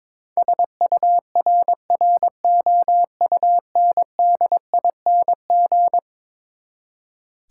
Morse code, heard as a keyed tone, about 22 wpm, 700 Hz. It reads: SURROUNDING
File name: practice / 22wpm / Eleven Letter Words - 22wpm